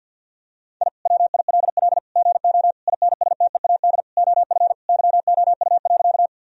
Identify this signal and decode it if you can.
Morse code, heard as a keyed tone, about 50 wpm, 700 Hz. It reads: I WILL GO INSTEAD OF YOU.
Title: Sentences from Top 1000 Words - Review Part 4 - 50wpm